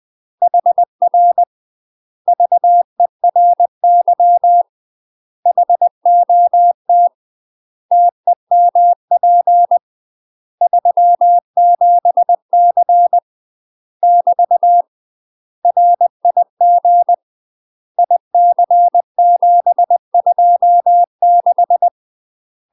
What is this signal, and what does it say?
Morse code, heard as a keyed tone, about 20 wpm, 700 Hz. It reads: HR VERY HOT TEMP 37C <BT> RIG IC726
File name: UK MASTER QSOs TRAINING / 20WPM / UK MASTER QSOs 0062